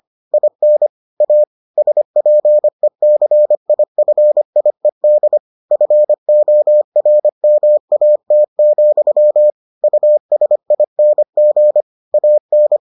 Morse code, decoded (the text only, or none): IN A SPECIFIED FORMAT, USING AN